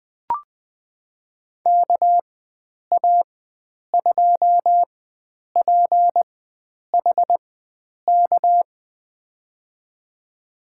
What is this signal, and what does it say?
Morse code, heard as a keyed tone, about 20 wpm, spaced as for 9 wpm, 700 Hz. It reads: KA2PHK
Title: USA Call Signs - 4x Character Spacing - 20wpm